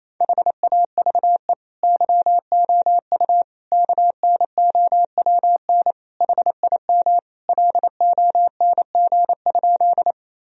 Morse code, heard as a keyed tone, about 28 wpm, 700 Hz. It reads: HAVE YOU KNOWN HIM LONG?
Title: Sentences from Top 1000 Words - Review Part 3 - 28wpm